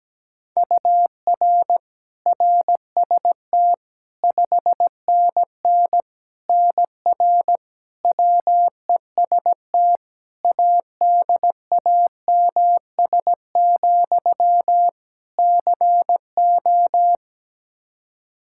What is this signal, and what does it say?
Morse code, heard as a keyed tone, about 17 wpm, 700 Hz. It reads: UR RST 5NN NR WEST ADAMS, CO
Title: Practice QSO - Exchange 1 - Beginning - 17wpm